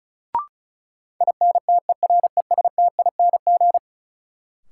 Morse code, heard as a keyed tone, about 34 wpm, 700 Hz. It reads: INTERESTING
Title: Sentences from Top 1000 Words - New Word Drill - 35wpm